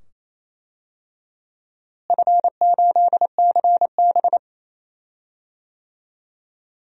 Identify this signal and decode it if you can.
Morse code, heard as a keyed tone, about 28 wpm, 700 Hz. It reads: F8CB
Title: DXCC-100 Call Signs - 4 Characters - Repeated 2x - 28wpm